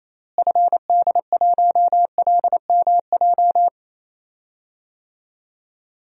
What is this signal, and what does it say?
Morse code, heard as a keyed tone, about 27 wpm, 700 Hz. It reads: FD1LMJ
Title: DXCC-100 Call Signs - 6 Characters - Repeated 2x - 28wpm